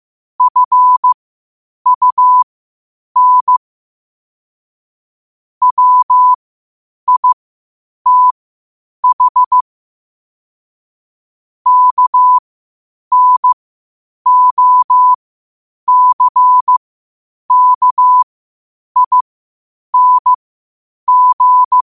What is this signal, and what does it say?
Morse code, heard as a keyed tone, about 15 wpm, spaced as for 8 wpm, 1000 Hz. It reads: FUN WITH KNOCKING
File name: qotdfiles / 8wpm